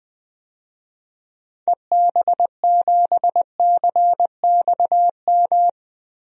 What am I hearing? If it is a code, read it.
Morse code, heard as a keyed tone, about 20 wpm, 700 Hz. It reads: EB7CXM
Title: DXCC-100 Call Signs - 6 Characters - Repeated 2x - 20wpm